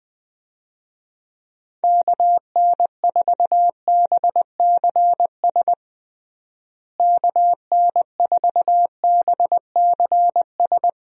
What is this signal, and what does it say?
Morse code, heard as a keyed tone, about 21 wpm, 700 Hz. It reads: KN4BCS KN4BCS
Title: USA Call Signs - 6 Characters - Repeated 2x - 20wpm